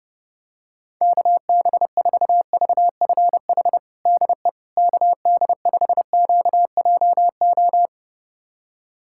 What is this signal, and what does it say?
Morse code, heard as a keyed tone, about 32 wpm, 700 Hz. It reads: KB4VFH DE KD5QJO
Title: USA Callsign DE Callsign 30wpm